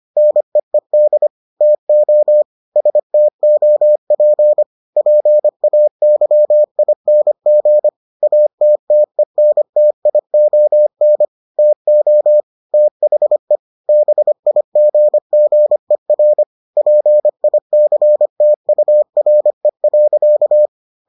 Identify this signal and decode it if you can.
Morse code, heard as a keyed tone, about 25 wpm, 600 Hz. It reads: NEED TO STOP PAYING ATTENTION TO THE BIGGER PICTURE.